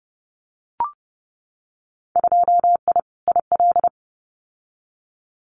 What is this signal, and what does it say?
Morse code, heard as a keyed tone, about 30 wpm, 700 Hz. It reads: 2I IL
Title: Winter Field Day Exchanges Encore 30wpm